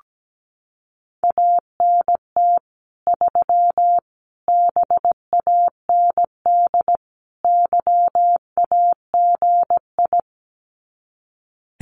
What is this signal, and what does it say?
Morse code, heard as a keyed tone, about 17 wpm, 700 Hz. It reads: ANT 3 BAND YAGI
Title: Common QSO Phrases 17wpm